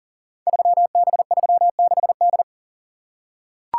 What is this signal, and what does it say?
Morse code, heard as a keyed tone, about 42 wpm, 700 Hz. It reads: 3B36D
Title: Code Groups - 5 Letter:Numbers - 40wpm